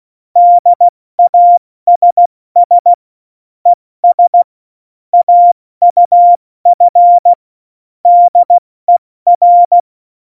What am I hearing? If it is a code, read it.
Morse code, heard as a keyed tone, about 16 wpm, 700 Hz. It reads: DASS ES AUF DER